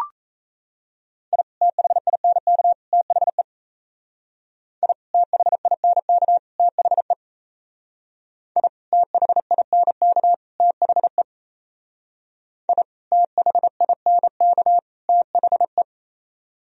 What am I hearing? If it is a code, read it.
Morse code, heard as a keyed tone, about 37 wpm, 700 Hz. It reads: I THINK THE I THINK THE I THINK THE I THINK THE